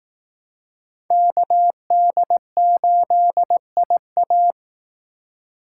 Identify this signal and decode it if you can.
Morse code, heard as a keyed tone, about 18 wpm, 700 Hz. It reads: KD8IA
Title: Speed Racing - USA 2x2 Call Signs 15wpm